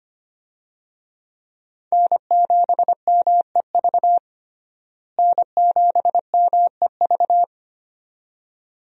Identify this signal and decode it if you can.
Morse code, heard as a keyed tone, about 25 wpm, 700 Hz. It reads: N7MEV N7MEV
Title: USA Call Signs - 5 Characters - Repeated 2x - 25wpm